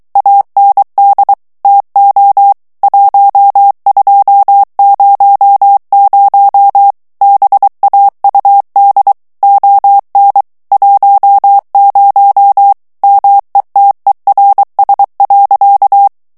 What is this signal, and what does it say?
Morse code, heard as a keyed tone, about 23 wpm, 800 Hz. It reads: AND TO 1200 BAUD ON 10 METERS.
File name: ANGLAIS texte clair 3 vitesse 22 mots minute